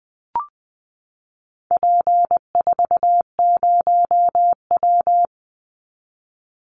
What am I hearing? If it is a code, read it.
Morse code, heard as a keyed tone, about 20 wpm, 700 Hz. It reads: P40W
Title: Call Signs Repeated 3x - 20wpm